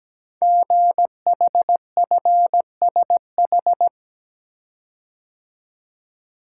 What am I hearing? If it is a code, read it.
Morse code, heard as a keyed tone, about 17 wpm, 700 Hz. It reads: GHFSH
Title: Code Groups - 5 Letters - 17wpm